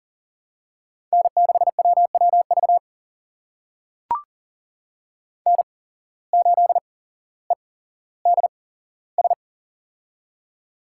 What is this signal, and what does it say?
Morse code, heard as a keyed tone, about 39 wpm, spaced as for 17 wpm, 700 Hz. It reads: N6WWV N8EDS
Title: USA Call Signs - 8x Character Spacing - 40wpm